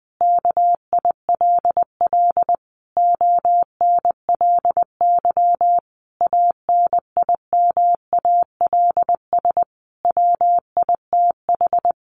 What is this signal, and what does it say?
Morse code, heard as a keyed tone, about 21 wpm, 700 Hz. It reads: KILL ONLY ANIMALS WITH
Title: A Sound Of Thunder 20wpm